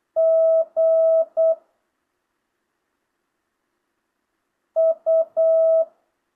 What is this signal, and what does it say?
Morse code, heard as a keyed tone, about 8 wpm, 660 Hz. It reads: G U